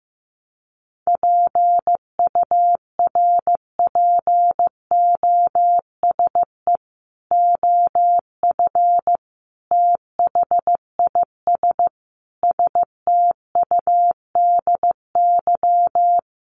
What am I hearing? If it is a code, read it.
Morse code, heard as a keyed tone, about 15 wpm, 700 Hz. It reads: PURPOSE OF THIS STUDY
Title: Sets of 5 Words 15wpm